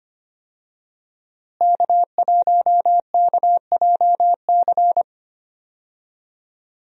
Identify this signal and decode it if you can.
Morse code, heard as a keyed tone, about 25 wpm, 700 Hz. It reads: K1KJC
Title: USA Call Signs - 5 Characters - Repeated 3x - 25wpm